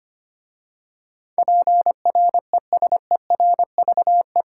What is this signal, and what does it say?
Morse code, heard as a keyed tone, about 25 wpm, 690 Hz. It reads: PRESERVE
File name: 8 Letter Words - First 3 Letters Given - 25wpm